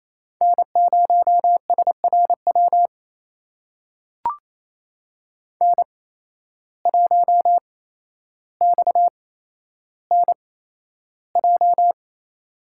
Morse code, heard as a keyed tone, about 28 wpm, spaced as for 12 wpm, 700 Hz. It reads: N0SRW N1XNJ
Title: USA Call Signs - 5 Characters - 8x Character Spacing - 28wpm